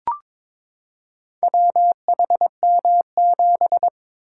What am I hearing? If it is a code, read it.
Morse code, heard as a keyed tone, about 23 wpm, 690 Hz. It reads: WHM7
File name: practice / 22wpm / Four Letter-Number Combinations - 22wpm